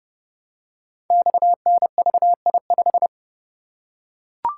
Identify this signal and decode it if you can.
Morse code, heard as a keyed tone, about 30 wpm, 700 Hz. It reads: XNVI5
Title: Code Groups - 5 Letter:Numbers - 30wpm